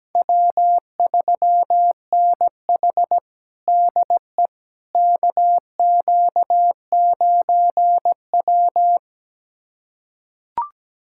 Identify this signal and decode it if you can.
Morse code, heard as a keyed tone, about 17 wpm, 700 Hz. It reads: W3NH DE KQ9W
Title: USA Callsign DE Callsign - 4 Characters - 17wpm